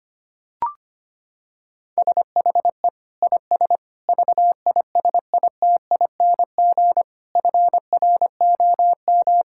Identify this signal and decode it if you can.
Morse code, heard as a keyed tone, about 25 wpm, 700 Hz. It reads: SHE IS VISITING FROM
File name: practice / 25wpm / Sentences with 4-Word Phrases - 25wpm